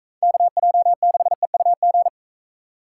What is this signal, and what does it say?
Morse code, heard as a keyed tone, about 42 wpm, 700 Hz. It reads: KJ6EUG